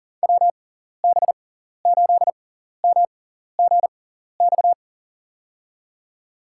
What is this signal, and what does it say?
Morse code, heard as a keyed tone, about 38 wpm, spaced as for 14 wpm, 700 Hz. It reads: WB8MGX
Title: USA Call Signs - 6 Characters - 6x Character Spacing - 40wpm